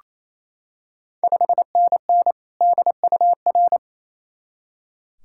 Morse code, heard as a keyed tone, about 28 wpm, 690 Hz. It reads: 5NN DUR
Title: Ontario QSO Party Exchange 28wpm